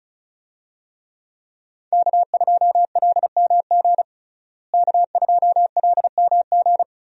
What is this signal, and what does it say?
Morse code, heard as a keyed tone, about 36 wpm, 700 Hz. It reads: K2LMG K2LMG
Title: USA Call Signs - 5 Characters - Repeated 3x - 35wpm